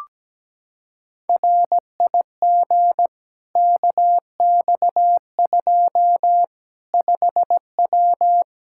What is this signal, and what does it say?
Morse code, heard as a keyed tone, about 17 wpm, 700 Hz. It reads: RIG KX2 5W
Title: Practice QSO - Exchange 2 - Info - 17wpm